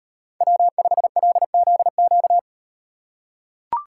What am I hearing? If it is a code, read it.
Morse code, heard as a keyed tone, about 35 wpm, 700 Hz. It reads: W5LZQ